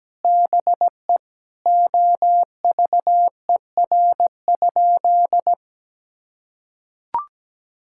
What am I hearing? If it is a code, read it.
Morse code, heard as a keyed tone, about 17 wpm, 700 Hz. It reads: BE OVER?
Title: Sentences from Top 1000 Words - Review Part 6 - 17wpm